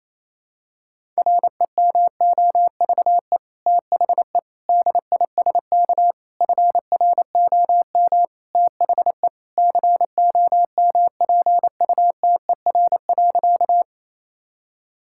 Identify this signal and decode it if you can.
Morse code, heard as a keyed tone, about 28 wpm, 700 Hz. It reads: REMOVE THE DISK FROM THE COMPUTER.
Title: Sentences from Top 1000 Words - Review Part 6 - 28wpm